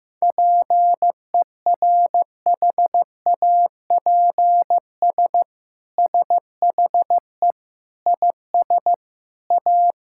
Morse code, heard as a keyed tone, about 15 wpm, 700 Hz. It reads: PERHAPS SHE IS A
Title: Sentences from Top 1000 Words - Review Part 5 - 15wpm